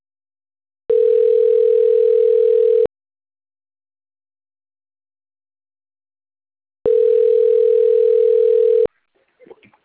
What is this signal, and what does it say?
Ringback tone (North America), 440 + 480 Hz.